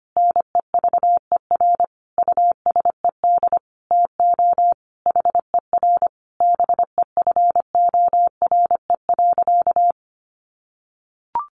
Morse code, heard as a keyed tone, about 25 wpm, 700 Hz. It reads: NEVER USED TO HER BEFORE.
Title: Sentences from Top 600 Words 25wpm